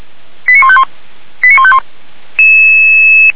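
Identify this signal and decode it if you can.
Five-tone selective call (ZVEI), heard as quick runs of five tones, tones of about 70 ms. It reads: 89151 89151
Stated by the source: ZVEI five-tone fire brigade call 85751 received on an SDR, logged by Feuerwehr Orlamünde.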